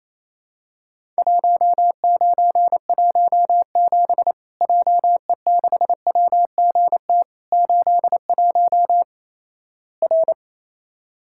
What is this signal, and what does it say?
Morse code, heard as a keyed tone, about 27 wpm, 700 Hz. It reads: 1917 JE6WGT 81 R
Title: Worked All Europe DX Contest - QTC Exchange - 28wpm